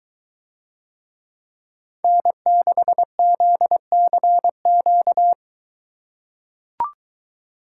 Morse code, heard as a keyed tone, about 23 wpm, 700 Hz. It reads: N6ZCQ